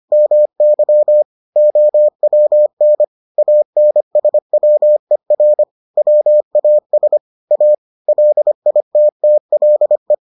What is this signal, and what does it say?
Morse code, heard as a keyed tone, about 25 wpm, 600 Hz. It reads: MY OWN ANSWER WAS A LITTLE